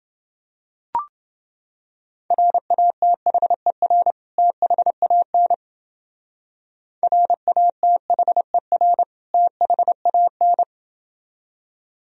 Morse code, heard as a keyed tone, about 31 wpm, 700 Hz. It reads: RATHER THAN RATHER THAN